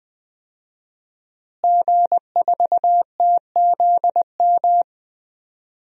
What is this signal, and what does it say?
Morse code, heard as a keyed tone, about 20 wpm, 700 Hz. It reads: G4TZM
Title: DXCC-100 Call Signs - 5 Characters - Repeated 2x - 20wpm